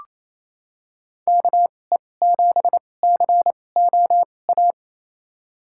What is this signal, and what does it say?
Morse code, heard as a keyed tone, about 28 wpm, spaced as for 20 wpm, 700 Hz. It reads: KE7COA
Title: USA Call Signs - 6 Characters - 2x Character Spacing - 28wpm